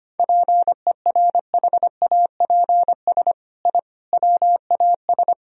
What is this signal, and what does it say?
Morse code, heard as a keyed tone, about 25 wpm, 700 Hz. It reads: PERHAPS I WAS
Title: Sentences from Top 1000 Words - Review Part 1 - 25wpm